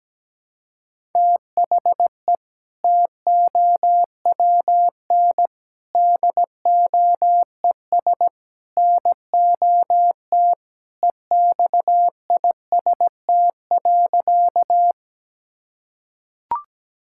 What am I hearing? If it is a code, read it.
Morse code, heard as a keyed tone, about 17 wpm, 700 Hz. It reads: THE TOWN DOES NOT EXIST.
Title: Sentences from Top 1000 Words - Review Part 5 - 17wpm